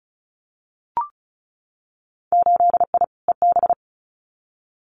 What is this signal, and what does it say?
Morse code, heard as a keyed tone, about 36 wpm, 700 Hz. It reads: 8I EB
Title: Winter Field Day Exchanges Encore 35wpm